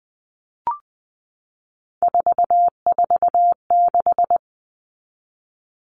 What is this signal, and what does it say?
Morse code, heard as a keyed tone, about 20 wpm, 700 Hz. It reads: 446
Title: Numbers - 3-Digit 20wpm